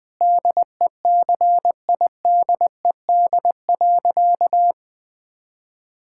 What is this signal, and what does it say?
Morse code, heard as a keyed tone, about 20 wpm, 700 Hz. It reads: DECIDED.
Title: Sentences with 3-Word Phrases - Spoken Beginning - 20wpm